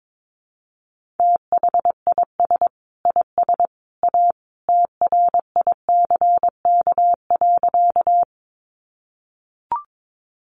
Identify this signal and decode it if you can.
Morse code, heard as a keyed tone, about 22 wpm, 700 Hz. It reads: THIS IS A TRICK.
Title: Sentences from Top 1500 Words 22wpm